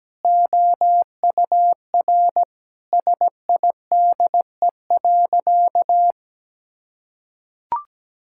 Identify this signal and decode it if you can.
Morse code, heard as a keyed tone, about 17 wpm, 700 Hz. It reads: OUR SIDE.
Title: Sentences from Top 1000 Words - Review Part 6 - 17wpm